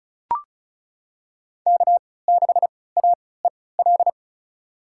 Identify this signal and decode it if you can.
Morse code, heard as a keyed tone, about 34 wpm, spaced as for 20 wpm, 700 Hz. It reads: K6AEL
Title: USA Call Signs - 5 Characters - 3x Character Spacing - 35wpm